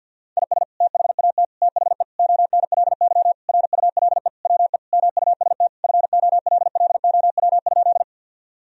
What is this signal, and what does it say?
Morse code, heard as a keyed tone, about 50 wpm, 700 Hz. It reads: IS THAT THE ONLY RULE WE MUST FOLLOW?
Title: Sentences from Top 1000 Words - Review Part 2 - 50wpm